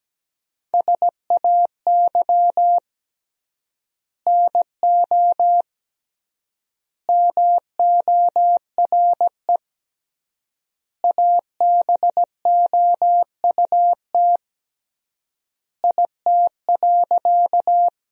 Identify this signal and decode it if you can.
Morse code, heard as a keyed tone, about 17 wpm, 700 Hz. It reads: SAY NO MORE ABOUT IT.
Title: 3x Word-Spacing - Sentences from Top 200 Words - 17wpm